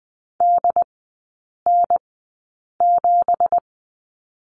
Morse code, heard as a keyed tone, about 20 wpm, spaced as for 8 wpm, 700 Hz. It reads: DN7
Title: 172 Lesson 28e - Number 9 - Three Characters 20wpm